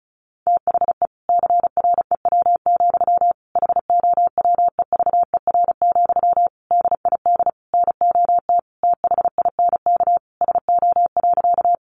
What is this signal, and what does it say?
Morse code, heard as a keyed tone, about 34 wpm, 700 Hz. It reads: THE CREW, HOWEVER, DID NOT THINK SO.
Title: Sentences from Top 1500 Words 35wpm